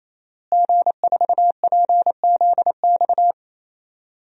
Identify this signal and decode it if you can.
Morse code, heard as a keyed tone, about 28 wpm, 700 Hz. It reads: G4PZX
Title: DXCC-100 Call Signs - 5 Characters - Repeated 3x - 28wpm